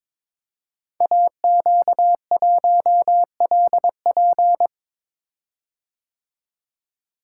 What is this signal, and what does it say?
Morse code, heard as a keyed tone, about 22 wpm, 700 Hz. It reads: AQ1LP
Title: Code Groups - 5 Letter:Numbers - 22wpm